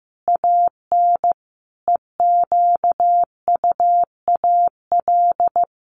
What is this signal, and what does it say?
Morse code, heard as a keyed tone, about 14 wpm, 700 Hz. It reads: AN EQUAL
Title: Sentences from Top 1400 Words 15wpm